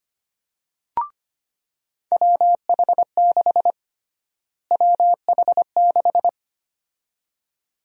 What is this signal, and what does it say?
Morse code, heard as a keyed tone, about 25 wpm, 700 Hz. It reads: WH6 WH6
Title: USA Call Sign Prefixes - Repeated 2x - 25wpm